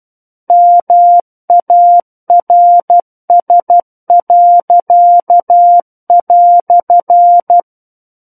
Morse code, heard as a keyed tone, about 12 wpm, 700 Hz. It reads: MARS."
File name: Book One / 03-Chapter 3 - 12 WPM